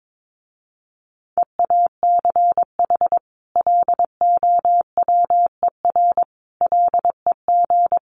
Morse code, heard as a keyed tone, about 22 wpm, 700 Hz. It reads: EACH LOWER LEG
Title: A Sound Of Thunder 22wpm